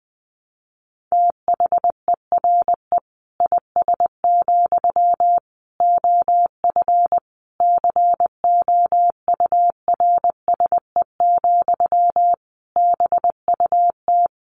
Morse code, heard as a keyed tone, about 20 wpm, 700 Hz. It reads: THERE IS, OF COURSE, BUT
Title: Sentences from Top 1200 Words 20wpm